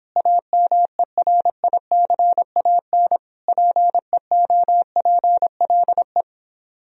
Morse code, heard as a keyed tone, about 30 wpm, 700 Hz. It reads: AMERICAN PEOPLE